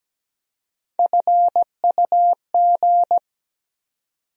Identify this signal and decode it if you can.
Morse code, heard as a keyed tone, about 17 wpm, 700 Hz. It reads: FUG
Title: USA Call Sign Postfixes - 3 Letters 17wpm